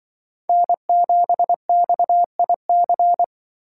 Morse code, heard as a keyed tone, about 24 wpm, 700 Hz. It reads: N7XIC
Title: Speed Racing - USA 1x3 Call Signs 20wpm